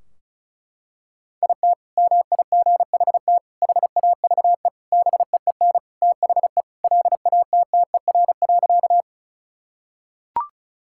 Morse code, heard as a keyed tone, about 36 wpm, 700 Hz. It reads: IT MIGHT HAVE BEEN THE LATTER.
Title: Sentences from Top 1600 Words 35wpm